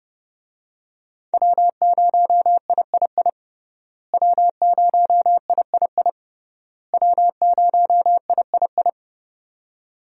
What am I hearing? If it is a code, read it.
Morse code, heard as a keyed tone, about 32 wpm, 700 Hz. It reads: W0III W0III W0III
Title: USA Call Signs - 5 Characters - Repeated 3x - 30wpm